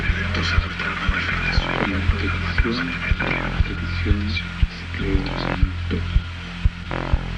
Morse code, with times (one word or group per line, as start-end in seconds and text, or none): none